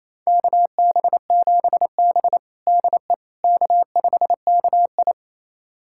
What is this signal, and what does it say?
Morse code, heard as a keyed tone, about 28 wpm, 700 Hz. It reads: KB7B DE K5KI
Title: USA Callsign DE Callsign - 4 Characters - 28wpm